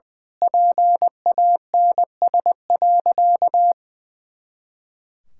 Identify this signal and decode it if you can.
Morse code, heard as a keyed tone, about 20 wpm, 700 Hz. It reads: PANS.
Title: Pangram Challenge v2 - 20wpm